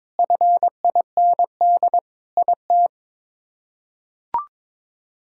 Morse code, heard as a keyed tone, about 22 wpm, 700 Hz. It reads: FIND IT
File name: practice / 22wpm / Sentences with 3-Word Phrases - 22wpm